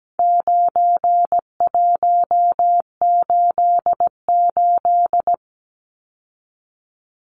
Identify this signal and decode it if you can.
Morse code, heard as a keyed tone, about 17 wpm, 700 Hz. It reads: 9188
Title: Numbers - 4-Digit 17wpm